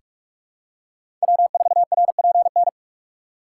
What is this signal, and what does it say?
Morse code, heard as a keyed tone, about 46 wpm, 690 Hz. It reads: W4RPN
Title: USA Call Signs - 5 Characters - 6x Character Spacing - 45wpm